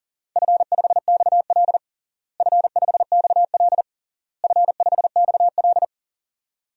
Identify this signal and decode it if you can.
Morse code, heard as a keyed tone, about 40 wpm, 700 Hz. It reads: F5XL F5XL F5XL
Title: DXCC-100 Call Signs - 4 Characters - Repeated 3x - 40wpm